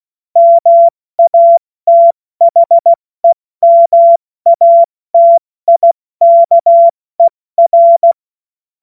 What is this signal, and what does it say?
Morse code, heard as a keyed tone, about 16 wpm, spaced as for 14 wpm, 680 Hz. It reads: MATHEMATIKER
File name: Morsekurs / Witze / Witz 26 Dezember 2023 70 F80